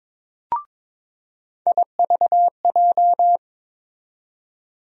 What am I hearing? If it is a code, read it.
Morse code, heard as a keyed tone, about 22 wpm, 700 Hz. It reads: IVJ
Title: USA Call Sign Postfixes - 3 Letters 22wpm